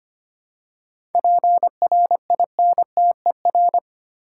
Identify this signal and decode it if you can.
Morse code, heard as a keyed tone, about 25 wpm, 700 Hz. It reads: PRINTER
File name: practice / 25wpm / Easy Words with Hints - 25wpm